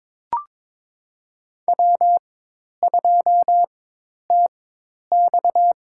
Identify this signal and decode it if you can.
Morse code, heard as a keyed tone, about 22 wpm, spaced as for 10 wpm, 700 Hz. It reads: W2TX
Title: USA Call Signs - 4 Characters - 4x Character Spacing - 22wpm